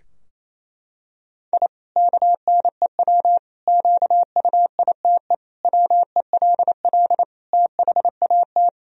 Morse code, heard as a keyed tone, about 28 wpm, 700 Hz. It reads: I KNEW QUITE WELL THAT
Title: Sentences from Top 1000 Words - Review Part 3 - 28wpm